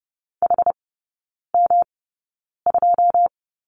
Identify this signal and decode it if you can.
Morse code, heard as a keyed tone, about 30 wpm, spaced as for 9 wpm, 700 Hz. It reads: HM2
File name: Lesson 18e - Letter m - Three Characters 30wpm